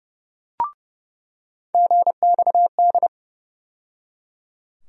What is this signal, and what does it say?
Morse code, heard as a keyed tone, about 30 wpm, 700 Hz. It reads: GXD
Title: USA Call Sign Postfixes - 3 Letters 30wpm